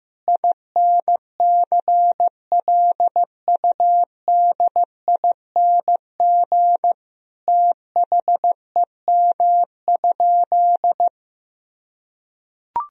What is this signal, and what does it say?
Morse code, heard as a keyed tone, about 15 wpm, 700 Hz. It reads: INCLUDING THEM?
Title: Sentences from Top 1000 Words - Review Part 4 - 15wpm